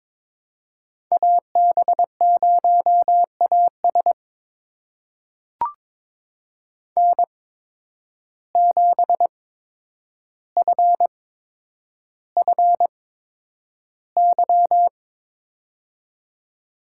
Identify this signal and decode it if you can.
Morse code, heard as a keyed tone, about 22 wpm, spaced as for 9 wpm, 700 Hz. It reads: AB0AS N7FFY
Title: USA Call Signs - 5 Characters - 8x Character Spacing - 22wpm